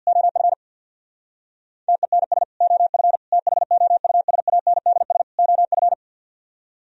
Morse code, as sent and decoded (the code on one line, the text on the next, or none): --- ..-. / - . -. ... / --- ..-. / - .... --- ..- ... .- -. -.. ... / --- ..-.
OF TENS OF THOUSANDS OF